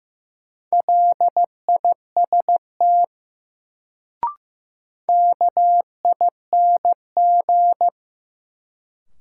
Morse code, heard as a keyed tone, about 15 wpm, 700 Hz. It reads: LIST KING
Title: Sentences from Top 400 Words - New Word Drill - 15wpm